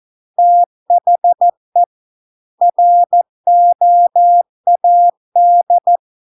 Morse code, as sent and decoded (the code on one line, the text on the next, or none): - .... / .-. --- .- -..
THE ROAD